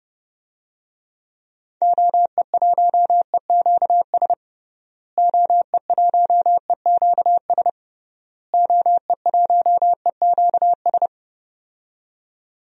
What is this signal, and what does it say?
Morse code, heard as a keyed tone, about 30 wpm, 700 Hz. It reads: OE1EQS OE1EQS OE1EQS
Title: DXCC-100 Call Signs - 6 Characters - Repeated 3x - 30wpm